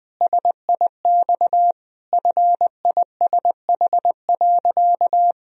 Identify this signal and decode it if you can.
Morse code, heard as a keyed tone, about 20 wpm, 700 Hz. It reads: SIX FISH.
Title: Sentences from Top 1000 Words - Review Part 2 - 20wpm